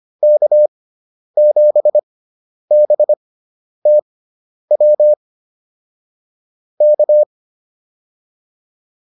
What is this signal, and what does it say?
Morse code, heard as a keyed tone, about 25 wpm, spaced as for 10 wpm, 600 Hz. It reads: K7BTW K